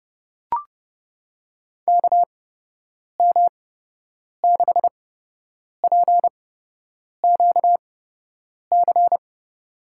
Morse code, heard as a keyed tone, about 32 wpm, spaced as for 8 wpm, 700 Hz. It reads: KM6PQC
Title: USA Call Signs - 6 Characters - 8x Character Spacing - 30wpm